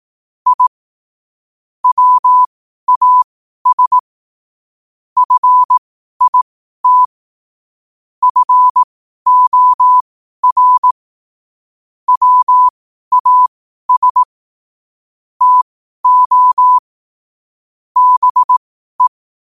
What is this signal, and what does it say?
Morse code, heard as a keyed tone, about 18 wpm, spaced as for 12 wpm, 1000 Hz. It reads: I WAS FIT FOR WAS TO BE
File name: qotdfiles / 12wpm